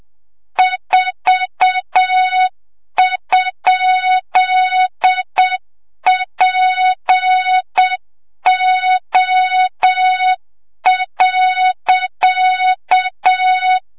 Morse code, heard as a keyed tone, about 7 wpm, 3700 Hz, with noise. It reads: 4?PO.
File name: Morze-HA2MN / hang / 25 GYAKORLAT